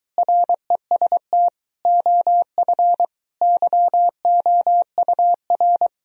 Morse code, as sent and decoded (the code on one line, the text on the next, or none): .-. ... - / --- ..-. / -.-- --- ..- .-.
REST OF YOUR